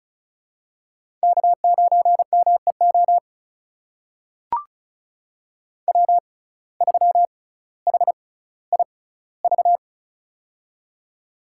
Morse code, heard as a keyed tone, about 36 wpm, spaced as for 18 wpm, 700 Hz. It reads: K9MEO W3HIV
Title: USA Call Signs - 5 Characters - 6x Character Spacing - 35wpm